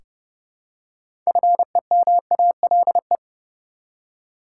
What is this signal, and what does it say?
Morse code, heard as a keyed tone, about 30 wpm, 700 Hz. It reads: FEMALE
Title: Six Letter Words - 30wpm